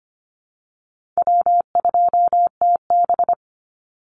Morse code, heard as a keyed tone, about 25 wpm, 700 Hz. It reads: W2TB
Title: Easy Call Signs Repeated 3x - 25wpm